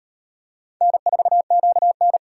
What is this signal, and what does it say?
Morse code, heard as a keyed tone, about 38 wpm, 700 Hz. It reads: N4QN